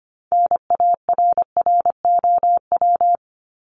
Morse code, heard as a keyed tone, about 25 wpm, 700 Hz. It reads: NARROW
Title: Top 1501-1750 Words 25wpm